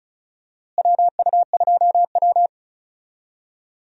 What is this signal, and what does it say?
Morse code, heard as a keyed tone, about 36 wpm, 700 Hz. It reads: WU2W
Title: USA Call Signs - 4 Characters - Repeated 3x - 35wpm